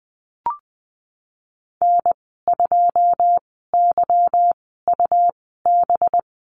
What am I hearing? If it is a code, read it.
Morse code, heard as a keyed tone, about 20 wpm, spaced as for 15 wpm, 700 Hz. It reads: N2YUB
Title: 186 Lesson 30g - Letter b - Call Signs 20wpm